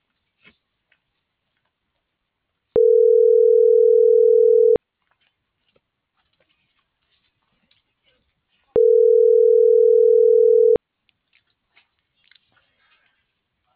Ringback tone (North America), 440 + 480 Hz.
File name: CENTRAL SAGOT / out diego.emir - .48040